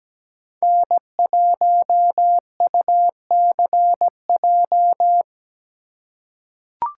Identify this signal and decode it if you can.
Morse code, heard as a keyed tone, about 17 wpm, 700 Hz. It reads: N1UCJ
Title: USA Call Signs - 5 Characters - 3x Character Spacing - 17wpm